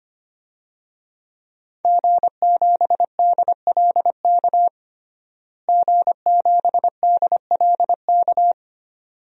Morse code, heard as a keyed tone, about 25 wpm, 700 Hz. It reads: G7DLK G7DLK
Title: DXCC-100 Call Signs - 5 Characters - Repeated 2x - 25wpm